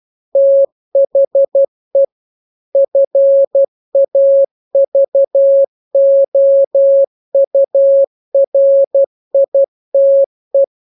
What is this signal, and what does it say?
Morse code, heard as a keyed tone, about 12 wpm, 550 Hz. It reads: THE FAVOURITE